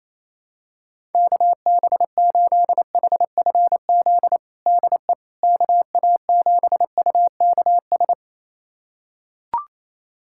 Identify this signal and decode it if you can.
Morse code, heard as a keyed tone, about 28 wpm, 700 Hz. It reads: KB8HFZ DE KA7UKS
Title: USA Callsign DE Callsign - 6 Characters - 28wpm